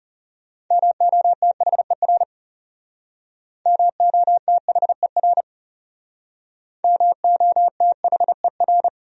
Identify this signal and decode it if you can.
Morse code, heard as a keyed tone, about 38 wpm, 700 Hz. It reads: MOTHER MOTHER MOTHER